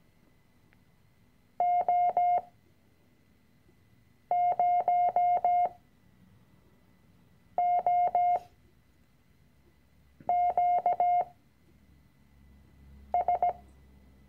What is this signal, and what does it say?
Morse code, heard as a keyed tone, about 17 wpm, spaced as for 4 wpm, 710 Hz. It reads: O0OQS
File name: CW-Online-Kurs / Anfänger / Lektion 006